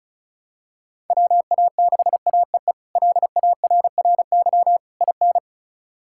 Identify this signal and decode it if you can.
Morse code, heard as a keyed tone, about 34 wpm, 700 Hz. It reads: WA6AEE LARRY IN
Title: K1USN SST QSO Practice - 35wpm